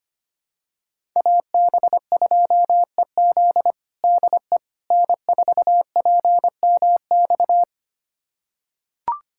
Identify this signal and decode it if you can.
Morse code, heard as a keyed tone, about 25 wpm, 700 Hz. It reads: AB2EZ DE N4PMX
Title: USA Callsign DE Callsign - 5 Characters - 25wpm